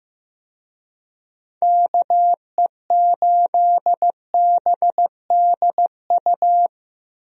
Morse code, heard as a keyed tone, about 15 wpm, 700 Hz. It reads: KE8BDU
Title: USA Call Signs - 6 Characters - Repeated 3x - 15wpm